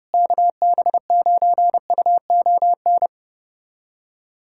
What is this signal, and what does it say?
Morse code, heard as a keyed tone, about 30 wpm, 700 Hz. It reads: KB9UON
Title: USA Callsign DE Callsign - 6 Characters - 30wpm